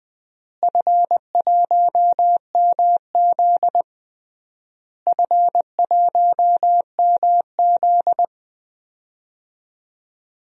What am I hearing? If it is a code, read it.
Morse code, heard as a keyed tone, about 20 wpm, 700 Hz. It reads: F1MZ F1MZ
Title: DXCC-100 Call Signs - 4 Characters - Repeated 3x - 20wpm